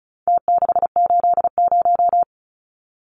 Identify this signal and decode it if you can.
Morse code, heard as a keyed tone, about 34 wpm, 700 Hz. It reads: T680
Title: Numbers - Contest Style - 4-Digit 35wpm